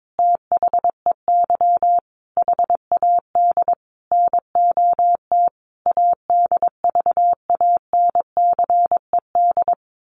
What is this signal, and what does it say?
Morse code, heard as a keyed tone, about 22 wpm, 700 Hz. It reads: THEY HAD NOT ADVANCED
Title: Sentences from Top 1200 Words 22wpm